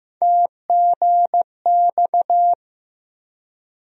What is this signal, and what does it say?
Morse code, heard as a keyed tone, about 15 wpm, 700 Hz. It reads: TGX
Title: USA Call Sign Postfixes - 3 Letters 15wpm